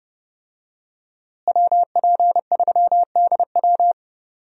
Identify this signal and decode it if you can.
Morse code, heard as a keyed tone, about 30 wpm, 700 Hz. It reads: WP3DW
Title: USA Call Signs - 5 Characters - Repeated 2x - 30wpm